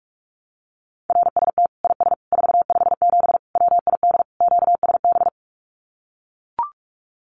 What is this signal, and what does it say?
Morse code, heard as a keyed tone, about 44 wpm, 700 Hz. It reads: RST IS 457 WID QSB
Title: Common QSO Phrases 45wpm